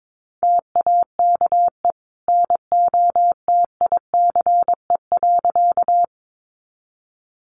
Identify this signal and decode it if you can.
Morse code, heard as a keyed tone, about 22 wpm, 700 Hz. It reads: TAKE NOTICE.
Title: Sentences from Top 400 Words 22wpm